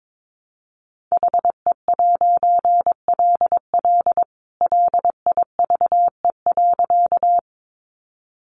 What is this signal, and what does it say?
Morse code, heard as a keyed tone, about 22 wpm, 700 Hz. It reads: HE'LL LIVE.
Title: A Sound Of Thunder 22wpm